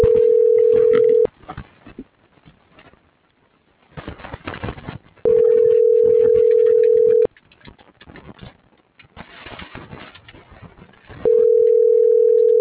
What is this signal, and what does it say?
Ringback tone (North America), 440 + 480 Hz.